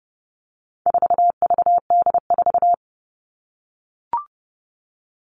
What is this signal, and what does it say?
Morse code, heard as a keyed tone, about 30 wpm, 700 Hz. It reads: 4VD4
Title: All Cut Numbers - 4-Digit 30wpm